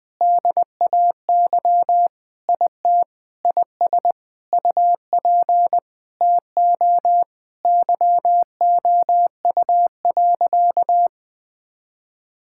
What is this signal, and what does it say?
Morse code, heard as a keyed tone, about 20 wpm, 700 Hz. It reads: DAY IT IS UP TO YOU.
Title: Sentences with 7-Word Phrases - 20wpm